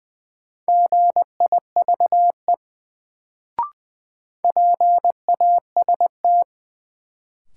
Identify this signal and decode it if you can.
Morse code, heard as a keyed tone, about 20 wpm, 700 Hz. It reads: GIVE PAST
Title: Sets of 2 Words - New Word Drill - 20wpm